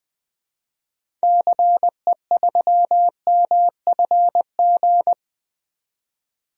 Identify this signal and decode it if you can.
Morse code, heard as a keyed tone, about 20 wpm, 700 Hz. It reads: CE3MFG
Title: DXCC-100 Call Signs - 6 Characters - 20wpm